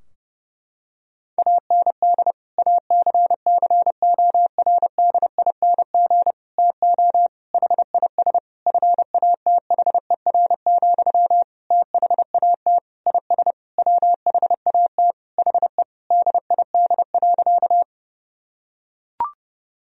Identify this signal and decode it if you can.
Morse code, heard as a keyed tone, about 30 wpm, 700 Hz. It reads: AND ACCORDING TO HIS FATHER, THAT IS WHAT HE DID.
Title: Sentences from Top 1000 Words - Review Part 4 - 30wpm